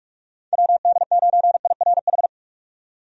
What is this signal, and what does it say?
Morse code, heard as a keyed tone, about 47 wpm, 700 Hz. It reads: WD9IRH
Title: USA Call Signs - 6 Characters - 45wpm